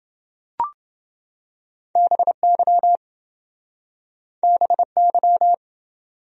Morse code, heard as a keyed tone, about 27 wpm, 700 Hz. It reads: BY BY